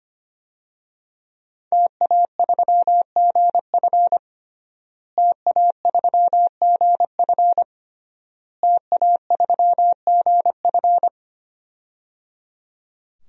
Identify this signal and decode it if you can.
Morse code, heard as a keyed tone, about 25 wpm, 700 Hz. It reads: TA3GF TA3GF TA3GF
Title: DXCC-100 Call Signs - 5 Characters - Repeated 3x - 25wpm